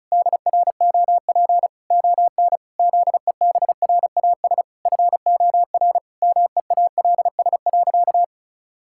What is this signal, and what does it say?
Morse code, heard as a keyed tone, about 36 wpm, 700 Hz. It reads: DROP ON ZEBRAS FOR MEALS.
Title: Pangram Challenge v2 - 35wpm